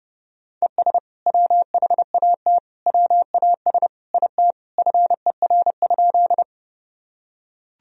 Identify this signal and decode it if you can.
Morse code, heard as a keyed tone, about 32 wpm, 700 Hz. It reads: ES WHAT WAS IT FER?
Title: Sentences from Top 100 Words and Abbreviations 30wpm